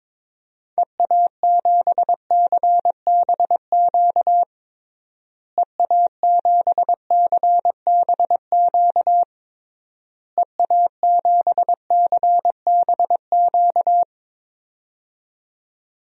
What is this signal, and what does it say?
Morse code, heard as a keyed tone, about 22 wpm, 700 Hz. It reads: EA7CBQ EA7CBQ EA7CBQ
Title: DXCC-100 Call Signs - 6 Characters - Repeated 3x - 22wpm